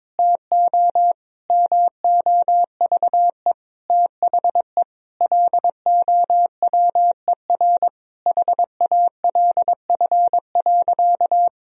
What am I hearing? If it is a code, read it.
Morse code, heard as a keyed tone, about 22 wpm, 700 Hz. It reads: TO MOVE THE LOWER HALF.
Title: Sentences from Top 1000 Words - Review Part 4 - 22wpm